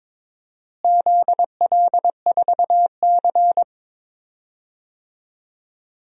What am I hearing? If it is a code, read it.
Morse code, heard as a keyed tone, about 22 wpm, 700 Hz. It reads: ZL4C
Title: DXCC-100 Call Signs - 4 Characters - Repeated 2x - 22wpm